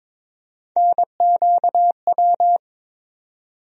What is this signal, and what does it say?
Morse code, heard as a keyed tone, about 21 wpm, 700 Hz. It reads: NQW
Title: USA Call Sign Postfixes - 3 Letters 22wpm